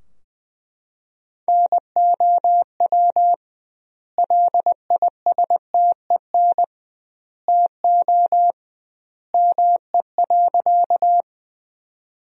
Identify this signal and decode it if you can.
Morse code, heard as a keyed tone, about 21 wpm, 700 Hz. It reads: NOW LISTEN TO ME.
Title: 2x Word-Spacing - Sentences from Top 400 Words - 20wpm